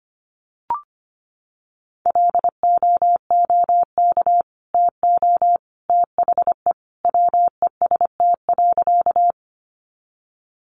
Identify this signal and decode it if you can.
Morse code, heard as a keyed tone, about 25 wpm, 700 Hz. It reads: LOOK TO THE WEST.
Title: Sentences from Top 400 Words 25wpm